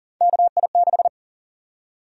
Morse code, heard as a keyed tone, about 40 wpm, 700 Hz. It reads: KI6
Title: USA Call Sign Prefixes - 40wpm